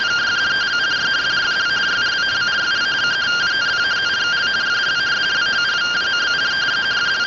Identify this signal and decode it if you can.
Two-tone teleprinter signal, about 170 Hz shift.
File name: findings / S ss-portrait.gen-05 rtty rx